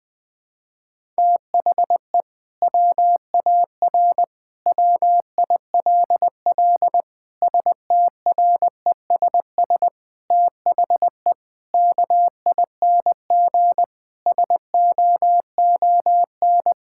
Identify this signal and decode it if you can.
Morse code, heard as a keyed tone, about 20 wpm, 700 Hz. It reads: THE WAR WILL STRESS THE KING SOON
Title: Sentences from Top 1000 Words - Review Part 6 - 20wpm